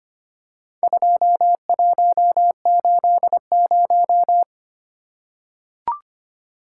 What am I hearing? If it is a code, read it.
Morse code, heard as a keyed tone, about 25 wpm, 700 Hz. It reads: 2180
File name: practice / 25wpm / Four Letter-Number Combinations - 25wpm